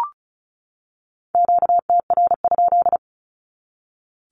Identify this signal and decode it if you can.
Morse code, heard as a keyed tone, about 34 wpm, 700 Hz. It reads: QTR?
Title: Q-Codes 35wpm